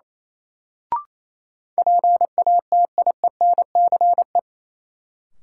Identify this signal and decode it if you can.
Morse code, heard as a keyed tone, about 28 wpm, 700 Hz. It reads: PATIENCE
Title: Sentences from Top 1300 Words - New Word Drill - 28wpm